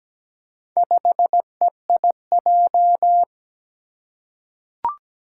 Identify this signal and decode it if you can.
Morse code, heard as a keyed tone, about 17 wpm, 700 Hz. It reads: 5EIJ
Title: Four Challenging Letter-Number Combinations 17wpm